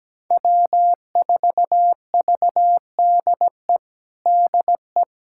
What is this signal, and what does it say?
Morse code, heard as a keyed tone, about 17 wpm, 700 Hz. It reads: W4VDE DE
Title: USA Callsign DE Callsign - 5 Characters - 17wpm